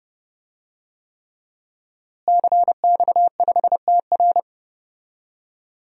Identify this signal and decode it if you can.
Morse code, heard as a keyed tone, about 30 wpm, 700 Hz. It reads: CX5TR
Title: DXCC-100 Call Signs - 5 Characters - Repeated 3x - 30wpm